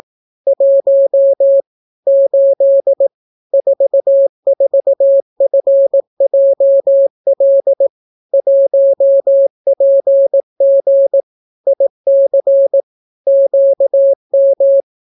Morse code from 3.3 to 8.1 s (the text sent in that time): 44FJL